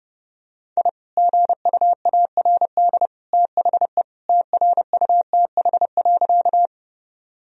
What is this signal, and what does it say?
Morse code, heard as a keyed tone, about 30 wpm, 700 Hz. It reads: I GUARD THE TRUTH.
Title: Sentences from Top 1000 Words - Review Part 4 - 30wpm